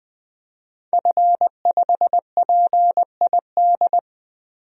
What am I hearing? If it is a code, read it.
Morse code, heard as a keyed tone, about 20 wpm, 700 Hz. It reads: F5PID
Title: DXCC-100 Call Signs - 5 Characters - 20wpm